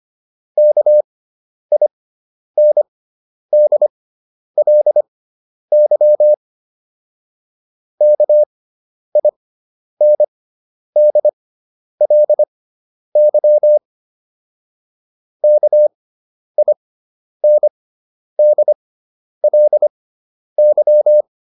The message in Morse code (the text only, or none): KINDLY KINDLY KINDLY